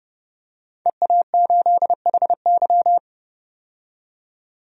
Morse code, heard as a keyed tone, about 30 wpm, 700 Hz. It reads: EA8HY
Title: DXCC-100 Call Signs - 5 Characters - 30wpm